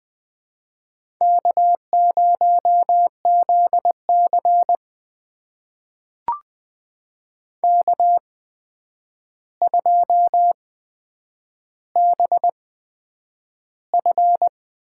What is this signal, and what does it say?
Morse code, heard as a keyed tone, about 20 wpm, spaced as for 9 wpm, 700 Hz. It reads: K0ZC K2BF
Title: USA Call Signs - 4 Characters - 8x Character Spacing - 20wpm